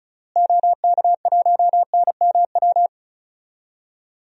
Morse code, heard as a keyed tone, about 34 wpm, 700 Hz. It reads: OK1NMW
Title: DXCC-100 Call Signs - 6 Characters - Repeated 3x - 35wpm